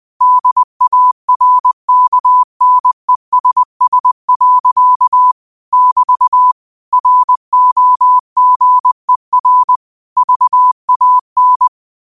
Morse code, heard as a keyed tone, about 20 wpm, 1000 Hz. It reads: DARKNESS. <BT> ROGER VAN
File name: qotdfiles / 20wpm